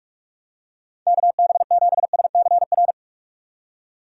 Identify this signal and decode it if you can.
Morse code, heard as a keyed tone, about 43 wpm, 700 Hz. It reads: KB7SCR
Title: USA Call Signs - 3x Character Spacing - 45wpm